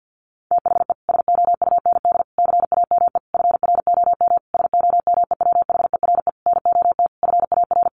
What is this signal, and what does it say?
Morse code, heard as a keyed tone, about 50 wpm, 700 Hz. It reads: THE SOUND CAME FROM SOMEWHERE NOT FAR
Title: Sentences from Top 1400 Words 50wpm